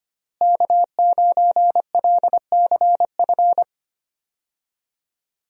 Morse code, heard as a keyed tone, about 25 wpm, 700 Hz. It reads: K9LCF
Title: USA Call Signs - 5 Characters - Repeated 2x - 25wpm